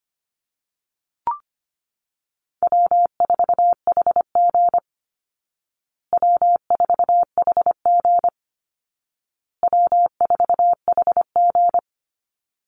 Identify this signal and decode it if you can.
Morse code, heard as a keyed tone, about 25 wpm, 700 Hz. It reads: W4HG W4HG W4HG
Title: Call Signs Repeated 2x - 25wpm